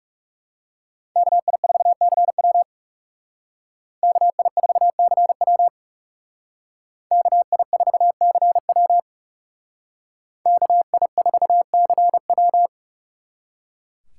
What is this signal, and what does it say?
Morse code, heard as a keyed tone, about 39 wpm, 700 Hz. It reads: KI4CW KI4CW KI4CW KI4CW